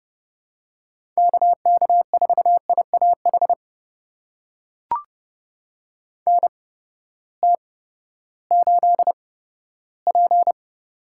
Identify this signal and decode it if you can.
Morse code, heard as a keyed tone, about 30 wpm, spaced as for 15 wpm, 700 Hz. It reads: KK4IAH NT8P
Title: USA Call Signs - 8x Character Spacing - 30wpm